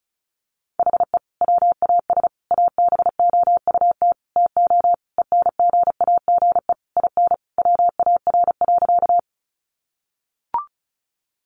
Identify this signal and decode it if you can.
Morse code, heard as a keyed tone, about 34 wpm, 700 Hz. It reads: HE WAS ABOUT TO ENGAGE IN WAR.
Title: Sentences from Top 1500 Words 35wpm